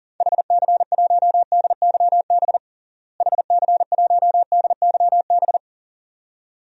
Morse code, heard as a keyed tone, about 40 wpm, 700 Hz. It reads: HC1DYB HC1DYB
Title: DXCC-100 Call Signs - 6 Characters - Repeated 2x - 40wpm